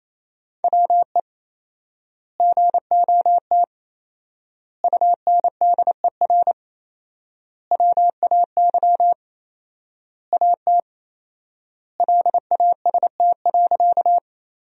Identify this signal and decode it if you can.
Morse code, heard as a keyed tone, about 28 wpm, 700 Hz. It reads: WE GOT UNDER WAY AT LAST.
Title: 4x Word-Spacing - Sentences from Top 300 Words - 28wpm